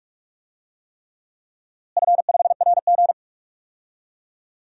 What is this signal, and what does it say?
Morse code, heard as a keyed tone, about 41 wpm, 700 Hz. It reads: F5RG
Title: DXCC-100 Call Signs - 4 Characters - Repeated 3x - 45wpm